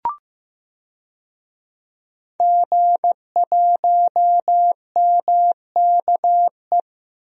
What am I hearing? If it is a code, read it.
Morse code, heard as a keyed tone, about 15 wpm, 700 Hz. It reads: G1MKE
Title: DXCC-100 Call Signs - 5 Characters - Repeated 3x - 15wpm